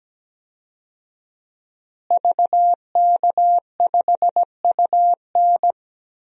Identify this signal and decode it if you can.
Morse code, heard as a keyed tone, about 17 wpm, 700 Hz. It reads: VK5UN
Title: DXCC-100 Call Signs - 5 Characters - Repeated 3x - 17wpm